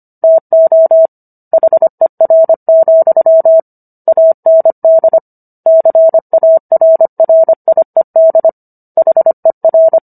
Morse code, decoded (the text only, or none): TO HER, AND CARRIED HER